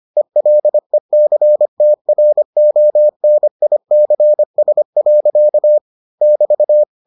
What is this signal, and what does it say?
Morse code, heard as a keyed tone, about 25 wpm, 600 Hz. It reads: ELECTRONICS. <BT>